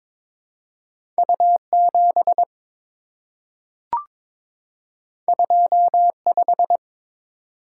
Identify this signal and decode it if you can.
Morse code, heard as a keyed tone, about 22 wpm, 700 Hz. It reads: U7 25
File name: practice / 22wpm / Two Letter-Number Combinations - 22wpm